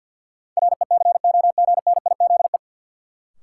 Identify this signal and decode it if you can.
Morse code, heard as a keyed tone, about 50 wpm, 700 Hz. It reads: RECOGNIZE
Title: Sentences from Top 1400 Words - New Word Drill Alt - 50wpm